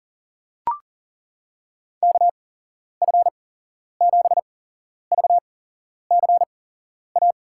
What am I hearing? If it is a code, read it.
Morse code, heard as a keyed tone, about 40 wpm, spaced as for 11 wpm, 700 Hz. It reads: KF7VCA
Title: USA Call Signs - 8x Character Spacing - 40wpm